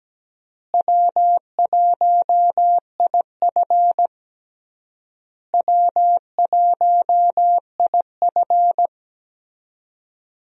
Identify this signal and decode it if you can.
Morse code, heard as a keyed tone, about 17 wpm, 700 Hz. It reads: W1IF W1IF
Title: USA Call Signs - 4 Characters - Repeated 3x - 17wpm